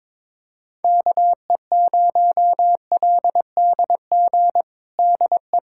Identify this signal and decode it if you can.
Morse code, heard as a keyed tone, about 22 wpm, 700 Hz. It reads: KE0LDG DE
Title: USA Callsign DE Callsign - 6 Characters - 22wpm